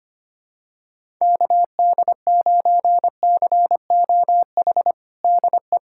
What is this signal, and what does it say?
Morse code, heard as a keyed tone, about 25 wpm, 700 Hz. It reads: KD9COH DE
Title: USA Callsign DE Callsign - 6 Characters - 25wpm